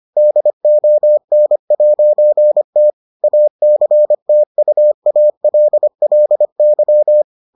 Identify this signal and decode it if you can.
Morse code, heard as a keyed tone, about 25 wpm, 600 Hz. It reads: DON'T ACTUALLY